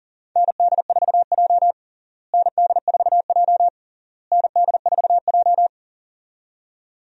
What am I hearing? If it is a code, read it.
Morse code, heard as a keyed tone, about 40 wpm, 700 Hz. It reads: ND4J ND4J ND4J
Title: USA Call Signs - 4 Characters - Repeated 3x - 40wpm